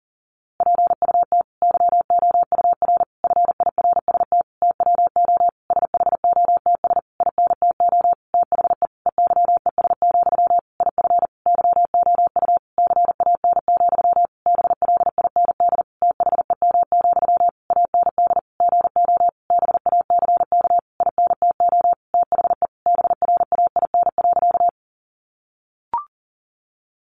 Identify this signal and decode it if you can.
Morse code, heard as a keyed tone, about 40 wpm, 700 Hz. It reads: PUT YOUR FIRST TWO SHOTS INTO THE EYES, IF YOU CAN, BLIND THEM, AND GO BACK INTO THE BRAIN.
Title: A Sound Of Thunder 40wpm